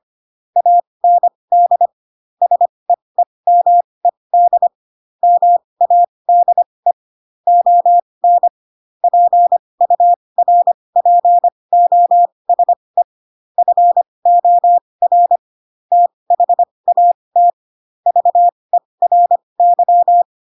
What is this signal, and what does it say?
Morse code, heard as a keyed tone, about 25 wpm, spaced as for 20 wpm, 700 Hz. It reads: AND SEEMED MADE ON PURPOSE FOR THAT VERY